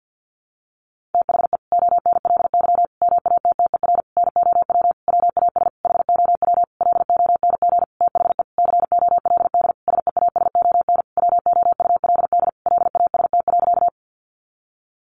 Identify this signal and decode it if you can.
Morse code, heard as a keyed tone, about 50 wpm, 700 Hz. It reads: THE ONLY MATTER NOW WAS HOW LONG THE COLD SEASON WOULD LAST.
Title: Sentences from Top 1400 Words 50wpm